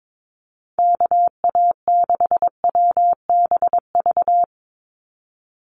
Morse code, heard as a keyed tone, about 22 wpm, 700 Hz. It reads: KA6WBV
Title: USA Call Signs 22wpm